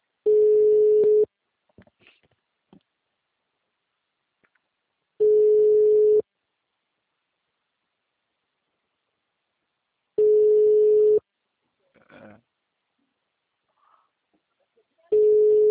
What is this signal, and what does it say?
Ringback tone (Europe, 425 Hz).